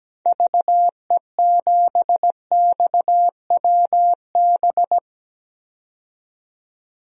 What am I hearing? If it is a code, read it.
Morse code, heard as a keyed tone, about 17 wpm, 700 Hz. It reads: VE7XWB
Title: DXCC-100 Call Signs - 6 Characters - 17wpm